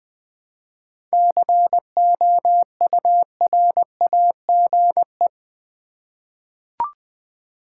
Morse code, heard as a keyed tone, about 20 wpm, 700 Hz. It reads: COURAGE
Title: Seven Letter Words - 20wpm